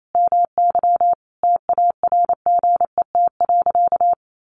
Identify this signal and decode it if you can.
Morse code, heard as a keyed tone, about 28 wpm, 700 Hz. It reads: MY TARGET.
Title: Sentences from Top 1200 Words 28wpm